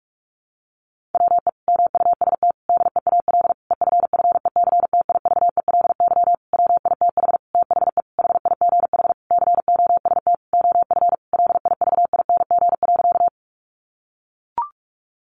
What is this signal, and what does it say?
Morse code, heard as a keyed tone, about 44 wpm, 700 Hz. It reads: WE MUST DEAL EFFECTIVELY WITH THE HIGH COST OF LIVING.
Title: Sentences from Top 1400 Words 45wpm